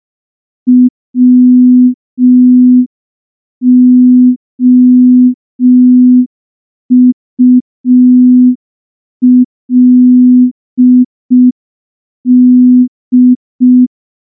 Morse code, heard as a keyed tone, about 5 wpm, 250 Hz. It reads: WOULD
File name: morse / morse-training-05WPM